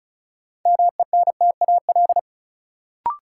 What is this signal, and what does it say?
Morse code, heard as a keyed tone, about 36 wpm, 700 Hz. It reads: MENTAL
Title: Sentences from Top 1000 Words - Review Part 4 - New Word Drill - 35wpm